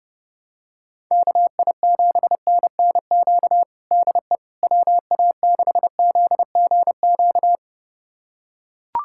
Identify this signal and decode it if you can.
Morse code, heard as a keyed tone, about 30 wpm, 700 Hz. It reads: KI7NNQ DE WA6ZGQ
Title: USA Callsign DE Callsign - 6 Characters - 30wpm